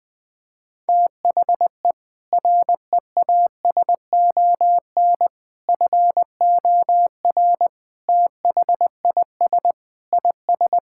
Morse code, heard as a keyed tone, about 20 wpm, 700 Hz. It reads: THE REASON FOR THIS IS
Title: Sentences with 7-Word Phrases - Spoken Beginning - 20wpm